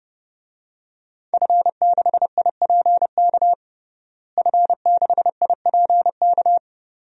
Morse code, heard as a keyed tone, about 30 wpm, 700 Hz. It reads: F6IPK F6IPK
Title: DXCC-100 Call Signs - 5 Characters - Repeated 2x - 30wpm